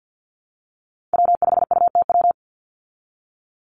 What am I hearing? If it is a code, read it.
Morse code, heard as a keyed tone, about 48 wpm, 700 Hz. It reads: W5UTW
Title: Call Signs Repeated 3x - 50wpm